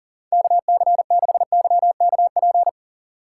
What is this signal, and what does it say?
Morse code, heard as a keyed tone, about 39 wpm, 700 Hz. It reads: KC6YKP
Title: USA Call Signs - 6 Characters - 6x Character Spacing - 40wpm